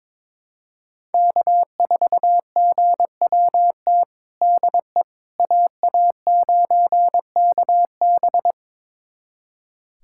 Morse code, heard as a keyed tone, about 22 wpm, 700 Hz. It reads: K4GWT DE AA9KB
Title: USA Callsign DE Callsign 22wpm